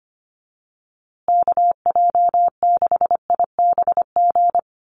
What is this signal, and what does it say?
Morse code, heard as a keyed tone, about 25 wpm, 700 Hz. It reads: KJ6IBG
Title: USA Call Signs Repeated 2x 25wpm